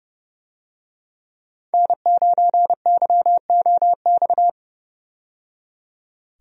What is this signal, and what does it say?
Morse code, heard as a keyed tone, about 30 wpm, 700 Hz. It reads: N9YOX
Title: USA Call Signs - 5 Characters - 30wpm